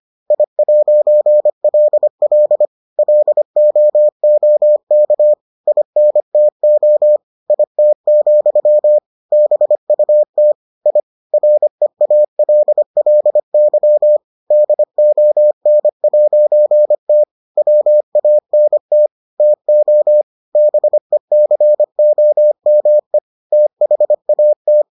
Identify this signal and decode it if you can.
Morse code, heard as a keyed tone, about 25 wpm, 600 Hz. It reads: I'LL LOOK INTO IT, BUT I REALLY DON'T WANT TO BECOME THAT